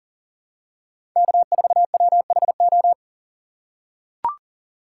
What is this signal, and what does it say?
Morse code, heard as a keyed tone, about 40 wpm, 700 Hz. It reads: K4WHO
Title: USA Call Signs - 5 Characters - 3x Character Spacing - 40wpm